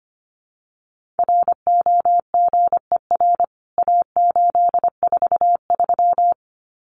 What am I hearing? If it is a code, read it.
Morse code, heard as a keyed tone, about 25 wpm, 700 Hz. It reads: ROGER A843
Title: CWT Exchange 25wpm